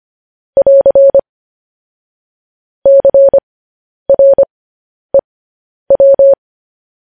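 Morse code, heard as a keyed tone, about 25 wpm, spaced as for 10 wpm, 550 Hz. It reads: <AR> CREW
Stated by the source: Morse code news headlines sent at 10 wpm.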